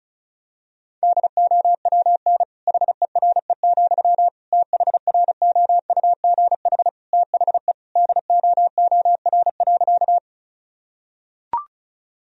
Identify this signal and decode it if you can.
Morse code, heard as a keyed tone, about 36 wpm, 700 Hz. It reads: DOWN HERE, THROUGH THE DOOR.
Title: Sentences from Top 1000 Words - Review Part 1 - 35wpm